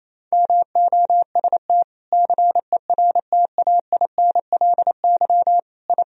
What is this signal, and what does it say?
Morse code, heard as a keyed tone, about 28 wpm, 700 Hz. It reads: MOST CERTAINLY I
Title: Sentences from Top 1000 Words - Review Part 1 - 28wpm